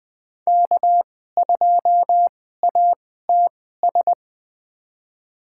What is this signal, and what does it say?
Morse code, heard as a keyed tone, about 20 wpm, spaced as for 14 wpm, 700 Hz. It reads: K2ATS
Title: USA Call Signs - 5 Characters - 2x Character Spacing - 20wpm